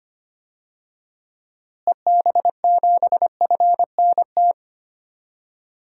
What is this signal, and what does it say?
Morse code, heard as a keyed tone, about 25 wpm, 700 Hz. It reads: EB7FNT
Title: DXCC-100 Call Signs - 6 Characters - Repeated 2x - 25wpm